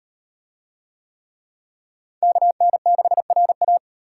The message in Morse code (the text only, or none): KN6RA